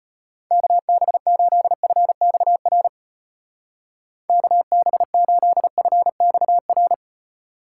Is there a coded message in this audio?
Yes: Morse code from 0.5 to 6.9 s, heard as a keyed tone, about 31 wpm, 700 Hz. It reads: KB8FXR KB8FXR